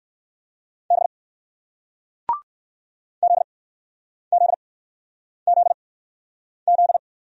Morse code, heard as a keyed tone, about 60 wpm, spaced as for 9 wpm, 700 Hz. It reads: B ZZZZ